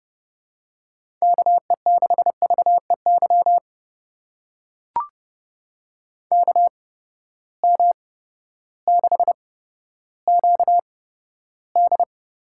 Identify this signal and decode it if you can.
Morse code, heard as a keyed tone, about 30 wpm, spaced as for 14 wpm, 700 Hz. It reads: KE6VEY KM6QD
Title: USA Call Signs - 6 Characters - 8x Character Spacing - 30wpm